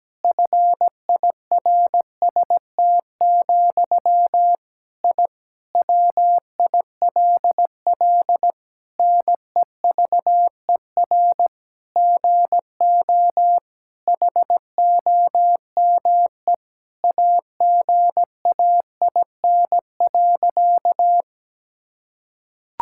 Morse code, heard as a keyed tone, about 17 wpm, 700 Hz. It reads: FIRST, I WILL NEVER GO HOME AGAIN.
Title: Sentences from Top 1000 Words - Review Part 6 - 17wpm